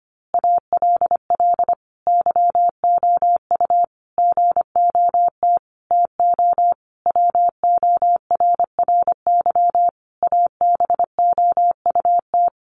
Morse code, heard as a keyed tone, about 25 wpm, 700 Hz. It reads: ALL YOU GOT TO WORRY ABOUT
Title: A Sound Of Thunder 25wpm